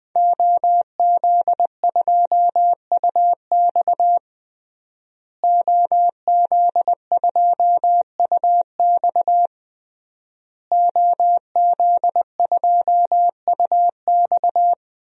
Morse code, heard as a keyed tone, about 20 wpm, 700 Hz. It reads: OZ2UX OZ2UX OZ2UX
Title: DXCC-100 Call Signs - 5 Characters - Repeated 3x - 20wpm